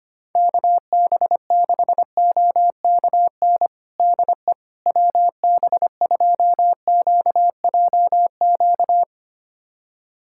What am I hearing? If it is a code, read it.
Morse code, heard as a keyed tone, about 25 wpm, 700 Hz. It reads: KB6OKN DE WB2QJQ
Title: USA Callsign DE Callsign - 6 Characters - 25wpm